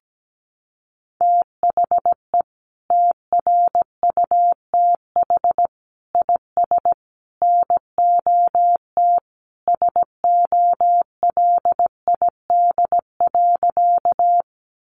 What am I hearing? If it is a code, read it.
Morse code, heard as a keyed tone, about 17 wpm, 700 Hz. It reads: THE TRUTH IS NOT SOLID.
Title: Sentences from Top 1500 Words 17wpm